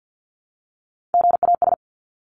Morse code, heard as a keyed tone, about 49 wpm, 700 Hz. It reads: GAS
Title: Top 1001-1250 Words 50wpm